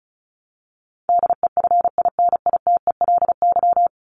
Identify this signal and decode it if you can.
Morse code, heard as a keyed tone, about 34 wpm, 700 Hz. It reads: DEFINITELY
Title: Top 1751-2000 Words 35wpm